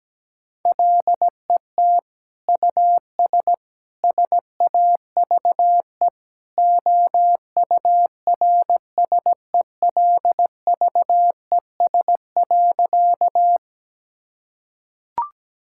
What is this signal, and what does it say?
Morse code, heard as a keyed tone, about 17 wpm, 700 Hz. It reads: LET US SAVE OURSELVES.
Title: Sentences from Top 1600 Words 17wpm